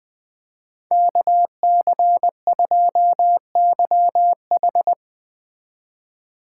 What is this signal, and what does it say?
Morse code, heard as a keyed tone, about 20 wpm, 700 Hz. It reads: KC2YH
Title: Speed Racing - USA 2x2 Call Signs 17wpm